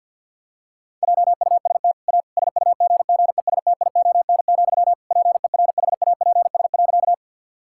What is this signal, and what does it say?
Morse code, heard as a keyed tone, about 50 wpm, 700 Hz. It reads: JUST A SUGGESTION, PERHAPS.
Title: Sentences from Top 1600 Words 50wpm